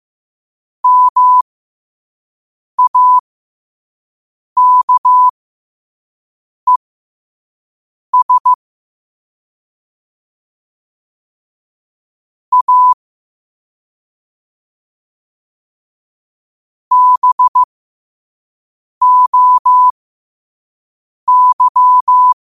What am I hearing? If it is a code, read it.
Morse code, heard as a keyed tone, about 15 wpm, spaced as for 5 wpm, 1000 Hz. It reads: MAKES A BOY
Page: Quotes for Tue, 12 Aug 2025 in Morse Code at 5 words per minute.